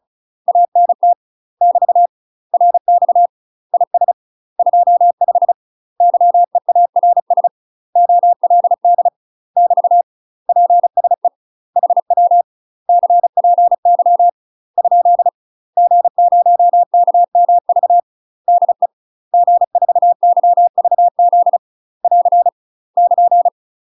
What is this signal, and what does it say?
Morse code, heard as a keyed tone, about 36 wpm, 700 Hz. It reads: ANT <BT> RX IS 25 YEARS OLD <BT> PSE HW CPY ? G0KMV DE G4YVZ <AR> <KN>
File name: UK MASTER QSOs TRAINING / 35WPM / UK MASTER QSOs 0022